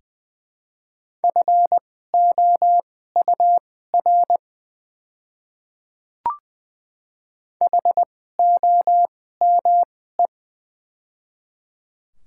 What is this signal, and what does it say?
Morse code, heard as a keyed tone, about 20 wpm, spaced as for 14 wpm, 700 Hz. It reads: FOUR HOME
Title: 2x Character Spacing - Top 300 Words - 20wpm